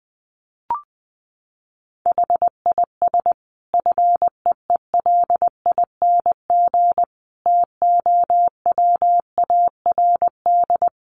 Morse code, heard as a keyed tone, about 20 wpm, 700 Hz. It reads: HIS FEELING TOWARD
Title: Sentences from Top 1300 Words 20wpm